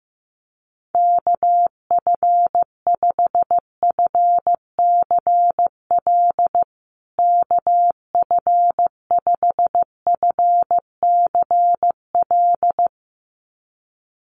Morse code, heard as a keyed tone, about 15 wpm, 700 Hz. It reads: KF5FCL KF5FCL
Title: USA Call Signs Repeated 2x 15wpm